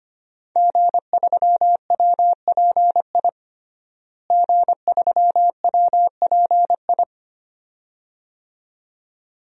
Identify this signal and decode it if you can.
Morse code, heard as a keyed tone, about 25 wpm, 700 Hz. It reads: G3WPI G3WPI
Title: DXCC-100 Call Signs - 5 Characters - Repeated 2x - 25wpm